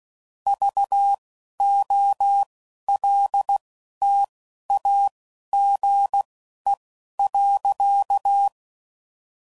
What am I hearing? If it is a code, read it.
Morse code, heard as a keyed tone, about 16 wpm, spaced as for 11 wpm, 800 Hz. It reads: VOLTAGE.
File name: CW Code Groups @10WPM / Technical Subjects / XL